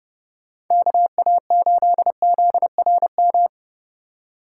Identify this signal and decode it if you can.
Morse code, heard as a keyed tone, about 29 wpm, 700 Hz. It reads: KA8ZRM